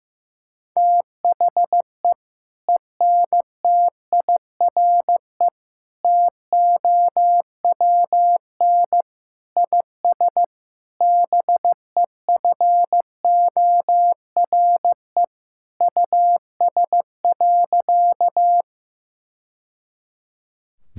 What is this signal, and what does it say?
Morse code, heard as a keyed tone, about 15 wpm, 700 Hz. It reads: THE ENTIRE TOWN IS BEFORE US.
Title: Sentences from Top 1000 Words - Review Part 5 - 15wpm